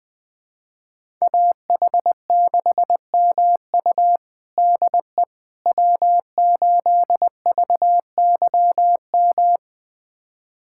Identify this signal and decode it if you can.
Morse code, heard as a keyed tone, about 21 wpm, 700 Hz. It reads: AH6MU DE W8VYM
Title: USA Callsign DE Callsign - 5 Characters - 20wpm